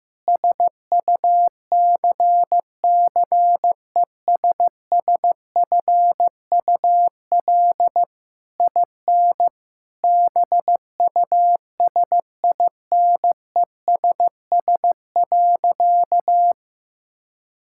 Morse code, heard as a keyed tone, about 15 wpm, 700 Hz. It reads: SUCCESSFUL IN BUSINESS.
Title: Sentences from Top 1000 Words - Review Part 6 - 15wpm